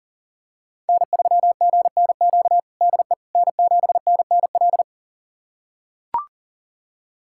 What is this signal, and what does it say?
Morse code, heard as a keyed tone, about 40 wpm, 700 Hz. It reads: N3GNQ DE N7NNL
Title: USA Callsign DE Callsign - 5 Characters - 40wpm